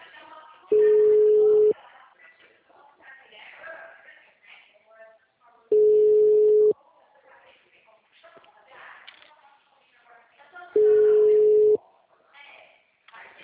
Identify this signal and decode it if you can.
Ringback tone (Europe, 425 Hz), with speech.